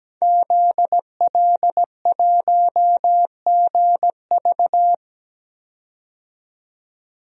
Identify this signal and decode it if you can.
Morse code, heard as a keyed tone, about 17 wpm, 700 Hz. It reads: ZL1GV
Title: DXCC-100 Call Signs - 5 Characters - 17wpm